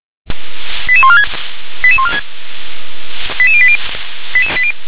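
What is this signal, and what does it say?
Five-tone selective call (ZVEI), heard as quick runs of five tones, tones of about 70 ms. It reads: 88146 88146 88080
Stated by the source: ZVEI five-tone call 88762 received on an SDR.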